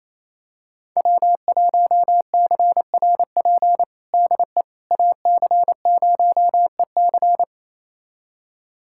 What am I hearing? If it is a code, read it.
Morse code, heard as a keyed tone, about 28 wpm, 700 Hz. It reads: W1CRP DE AC0EC
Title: USA Callsign DE Callsign - 5 Characters - 28wpm